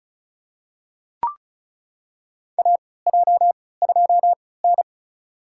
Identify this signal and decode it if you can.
Morse code, heard as a keyed tone, about 36 wpm, spaced as for 20 wpm, 700 Hz. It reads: AJ2N
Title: USA Call Signs - 4 Characters - 3x Character Spacing - 35wpm